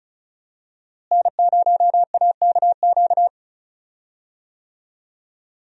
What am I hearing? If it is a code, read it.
Morse code, heard as a keyed tone, about 36 wpm, 700 Hz. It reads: N0AKQ
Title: USA Call Signs - 5 Characters - Repeated 3x - 35wpm